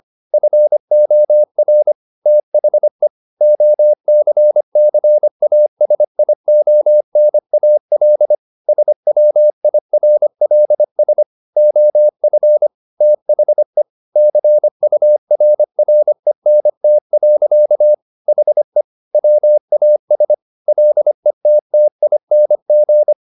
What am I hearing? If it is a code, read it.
Morse code, heard as a keyed tone, about 25 wpm, 600 Hz. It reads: FOR THE OCCASIONAL SWIRLS OF THE CURRENT. HE WAS LETTING
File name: the old man and the sea part 1 25wpm